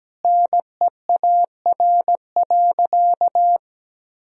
Morse code, heard as a keyed tone, about 17 wpm, 700 Hz. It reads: NEAR.
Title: Sentences from Top 1000 Words - Review Part 1 - 17wpm